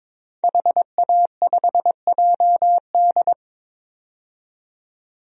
Morse code, heard as a keyed tone, about 22 wpm, 700 Hz. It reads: HA5JD
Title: DXCC-100 Call Signs - 5 Characters - 22wpm